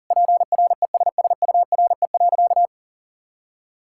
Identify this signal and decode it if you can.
Morse code, heard as a keyed tone, about 40 wpm, 700 Hz. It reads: PRESSURE.
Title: Sentences from Top 1000 Words - Review Part 6 - 40wpm